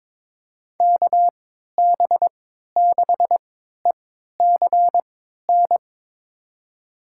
Morse code, heard as a keyed tone, about 22 wpm, spaced as for 13 wpm, 700 Hz. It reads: KB6ECN
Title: USA Call Signs - 6 Characters - 3x Character Spacing - 22wpm